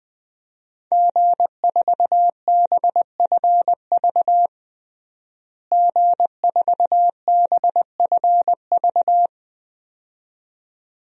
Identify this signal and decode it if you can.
Morse code, heard as a keyed tone, about 20 wpm, 700 Hz. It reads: G4BFV G4BFV
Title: DXCC-100 Call Signs - 5 Characters - Repeated 2x - 20wpm